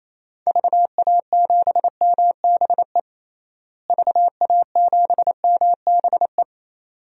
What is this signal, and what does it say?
Morse code, heard as a keyed tone, about 28 wpm, 700 Hz. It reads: VA7MBE VA7MBE
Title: DXCC-100 Call Signs - 6 Characters - Repeated 2x - 28wpm